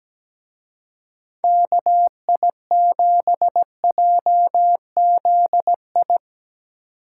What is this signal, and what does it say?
Morse code, heard as a keyed tone, about 18 wpm, 700 Hz. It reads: KI7JZI
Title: Speed Racing - USA 2x3 Call Signs 17wpm